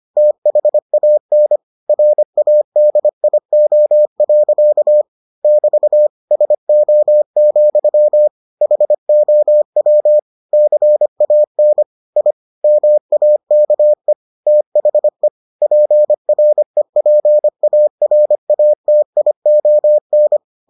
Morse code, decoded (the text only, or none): THAN RADIO. <BT> SO, HOW CAN I MAKE THE PREPARATION